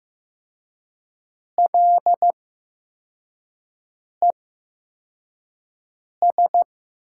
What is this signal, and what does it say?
Morse code, heard as a keyed tone, about 15 wpm, spaced as for 4 wpm, 700 Hz. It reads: LES
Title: USA Call Signs - 5 Characters - 8x Character Spacing - 15wpm